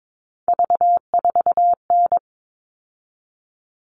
Morse code, heard as a keyed tone, about 22 wpm, 700 Hz. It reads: V4N
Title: All Cut Numbers - 3-Digit 22wpm